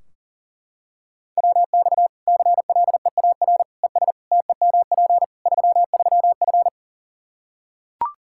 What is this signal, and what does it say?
Morse code, heard as a keyed tone, about 40 wpm, 700 Hz. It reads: WX CLEAR ES TEMP 33F
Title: Practice QSO - Exchange 2 - Info - 40wpm